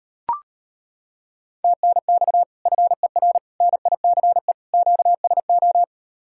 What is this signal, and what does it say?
Morse code, heard as a keyed tone, about 30 wpm, 700 Hz. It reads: TNX FER NICE QSO
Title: Speed-Racing Common QSO Phrases Encore - 25wpm